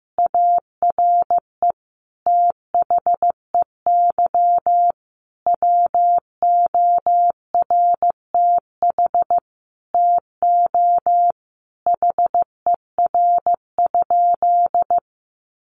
Morse code, heard as a keyed tone, about 15 wpm, 700 Hz. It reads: ARE THEY WORTH TO HER?
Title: Sentences from Top 800 Words 15wpm